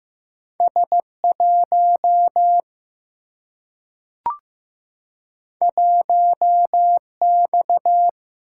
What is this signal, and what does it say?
Morse code, heard as a keyed tone, about 15 wpm, 700 Hz. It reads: S 1 1 X
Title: Single-Letter Single-Number 15wpm